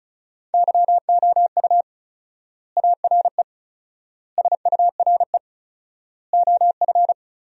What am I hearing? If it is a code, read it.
Morse code, heard as a keyed tone, about 36 wpm, 700 Hz. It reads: YOU ARE SURE OF
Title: 4x Word-Spacing - Sentences from Top 300 Words - 35wpm